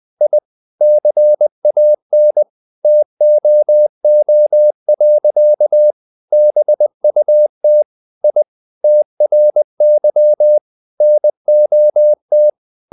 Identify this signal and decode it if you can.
Morse code, heard as a keyed tone, about 20 wpm, 600 Hz. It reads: I CAN TOO. BUT I TRY NOT
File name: the old man and the sea / the old man and the sea part 1 20wpm